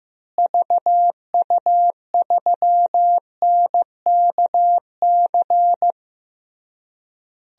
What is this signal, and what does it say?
Morse code, heard as a keyed tone, about 15 wpm, 700 Hz. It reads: VU3NKC
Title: DXCC-100 Call Signs - 6 Characters - Repeated 3x - 15wpm